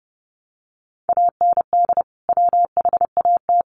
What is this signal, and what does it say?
Morse code, heard as a keyed tone, about 30 wpm, 700 Hz. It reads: AND WHAT
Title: Sentences from Top 300 Words 30wpm